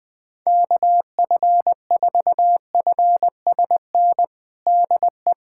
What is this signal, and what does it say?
Morse code, heard as a keyed tone, about 20 wpm, 700 Hz. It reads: KF4FSN DE
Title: USA Callsign DE Callsign - 6 Characters - 20wpm